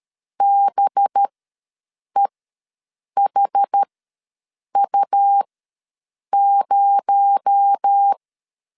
Morse code, heard as a keyed tone, about 13 wpm, spaced as for 7 wpm, 800 Hz. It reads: BEHU0
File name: LEZIONI / 6 PKX0 / 6 PKX0 ws 12wpm 3448